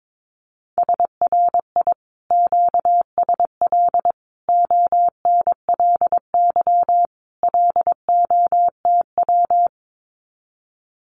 Morse code, heard as a keyed tone, about 22 wpm, 700 Hz. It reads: SRI QSL ONLY LOTW
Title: Common QSO Phrases Encore 22wpm